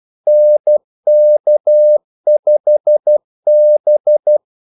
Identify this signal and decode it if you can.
Morse code, heard as a keyed tone, about 12 wpm, 600 Hz. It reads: NK5B
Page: 2 ekstra ordmellemrum mellem kaldesignalerne